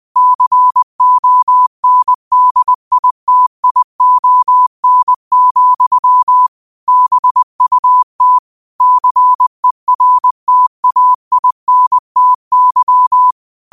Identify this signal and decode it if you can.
Morse code, heard as a keyed tone, about 20 wpm, 1000 Hz. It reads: CONDITION, BUT CERTAINTY